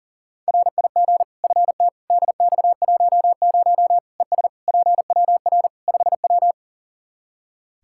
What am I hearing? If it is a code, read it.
Morse code, heard as a keyed tone, about 40 wpm, 700 Hz. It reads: RIG FT DX10 ES PWR 5W
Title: Practice QSO - Exchange 2 - Info - 40wpm